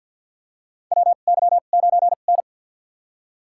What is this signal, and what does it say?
Morse code, heard as a keyed tone, about 51 wpm, spaced as for 36 wpm, 700 Hz. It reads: WY9N
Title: USA Call Signs - 2x Character Spacing - 50wpm